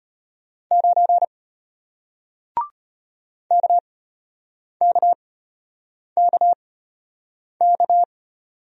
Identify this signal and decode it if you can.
Morse code, heard as a keyed tone, about 33 wpm, spaced as for 8 wpm, 700 Hz. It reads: 9 KKKK